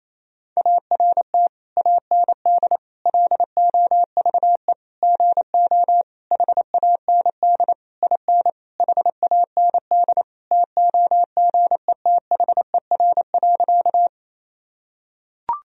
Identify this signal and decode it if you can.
Morse code, heard as a keyed tone, about 28 wpm, 700 Hz. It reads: ART AND LOVE GO HAND IN HAND TOGETHER.
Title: Sentences from Top 1000 Words - Review Part 3 - 28wpm